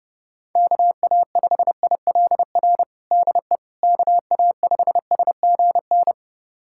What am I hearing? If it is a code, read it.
Morse code, heard as a keyed tone, about 30 wpm, 700 Hz. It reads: KA5ILR DE KA5SGN
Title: USA Callsign DE Callsign - 6 Characters - 30wpm